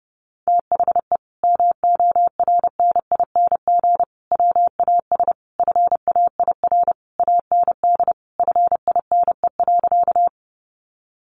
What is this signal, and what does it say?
Morse code, heard as a keyed tone, about 30 wpm, 700 Hz. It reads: THE MORNING WAS FAIR AND FINE.
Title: Sentences from Top 1300 Words 30wpm